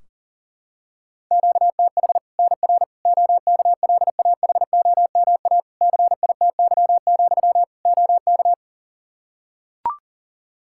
Morse code, heard as a keyed tone, about 40 wpm, 700 Hz. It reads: QTH NR OKLAHOMA CITY, OK
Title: Practice QSO - Exchange 1 - Beginning - 40wpm